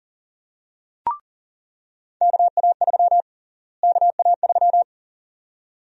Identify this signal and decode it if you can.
Morse code, heard as a keyed tone, about 40 wpm, 700 Hz. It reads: KA3 KA3
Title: USA Call Sign Prefixes - Repeated 2x - 40wpm